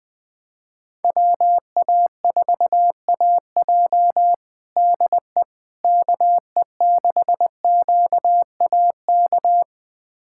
Morse code, heard as a keyed tone, about 21 wpm, 700 Hz. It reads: WA4AJ DE KE6QAK
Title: USA Callsign DE Callsign 20wpm